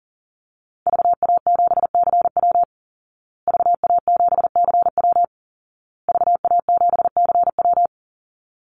Morse code, heard as a keyed tone, about 40 wpm, 700 Hz. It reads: VA7CW VA7CW VA7CW
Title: Call Signs Repeated 2x - 40wpm